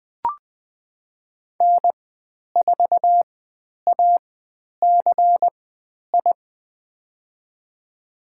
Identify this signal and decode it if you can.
Morse code, heard as a keyed tone, about 20 wpm, spaced as for 10 wpm, 700 Hz. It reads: N4ACI
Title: 095 Lesson 16g - Letter c - Call Signs 20wpm